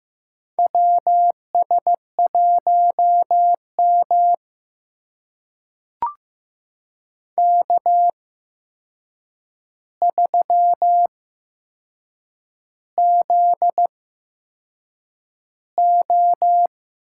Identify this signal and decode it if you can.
Morse code, heard as a keyed tone, about 15 wpm, spaced as for 6 wpm, 700 Hz. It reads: WS1M K3ZO
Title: USA Call Signs - 4 Characters - 8x Character Spacing - 15wpm